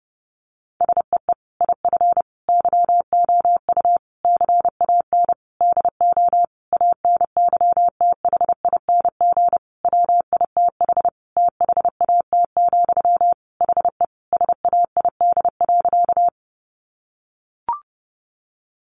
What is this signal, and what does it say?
Morse code, heard as a keyed tone, about 30 wpm, 700 Hz. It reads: SEE IF YOU CAN DO ANYTHING WITH THAT, HE SAID.
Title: Sentences from Top 1000 Words 30wpm